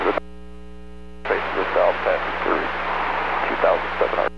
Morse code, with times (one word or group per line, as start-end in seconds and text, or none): none